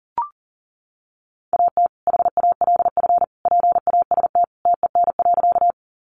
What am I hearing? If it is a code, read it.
Morse code, heard as a keyed tone, about 40 wpm, 700 Hz. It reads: AT HALF PAST TEN.
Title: Sentences from Top 600 Words 40wpm